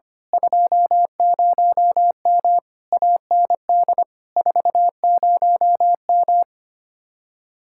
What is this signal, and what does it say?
Morse code, heard as a keyed tone, about 24 wpm, 700 Hz. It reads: 20M AND 40M
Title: Speed-Racing Common QSO Phrases - 25wpm